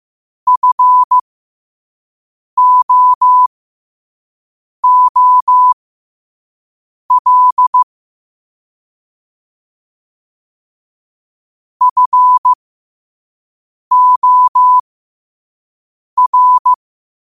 Morse code, heard as a keyed tone, about 15 wpm, spaced as for 5 wpm, 1000 Hz. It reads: FOOL FOR
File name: qotdfiles / 5wpm